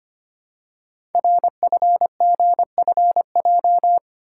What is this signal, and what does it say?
Morse code, heard as a keyed tone, about 25 wpm, 700 Hz. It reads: RFGFJ
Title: Code Groups - 5 Letters - 25wpm